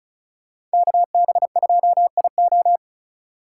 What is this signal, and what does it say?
Morse code, heard as a keyed tone, about 36 wpm, 700 Hz. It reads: KB2IO
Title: USA Call Signs - 5 Characters - 6x Character Spacing - 35wpm